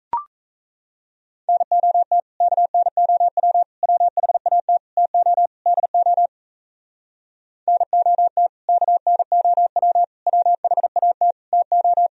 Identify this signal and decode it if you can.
Morse code, heard as a keyed tone, about 38 wpm, 700 Hz. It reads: NOT KNOW WHAT TO DO NOT KNOW WHAT TO